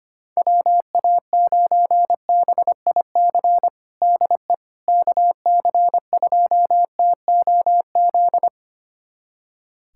Morse code, heard as a keyed tone, about 25 wpm, 700 Hz. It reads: WA9BIC DE KC2TOZ
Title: USA Callsign DE Callsign 25wpm